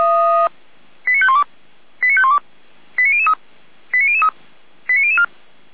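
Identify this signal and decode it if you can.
Five-tone selective call (ZVEI), heard as quick runs of five tones, tones of about 80 ms. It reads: 89512 89512 89003 89003 89004
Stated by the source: ZVEI five-tone call 89099 received by SDR at Feuerwehr Orlamünde.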